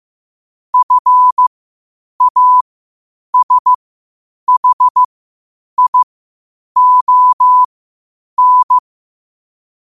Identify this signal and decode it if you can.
Morse code, heard as a keyed tone, about 15 wpm, spaced as for 8 wpm, 1000 Hz. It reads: FASHION
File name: qotdfiles / 8wpm